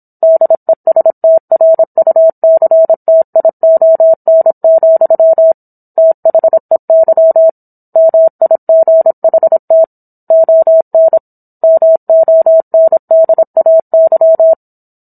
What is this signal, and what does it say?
Morse code, heard as a keyed tone, about 27 wpm, 650 Hz. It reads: DESTRUCTION, THEY MIGHT ON MONDAY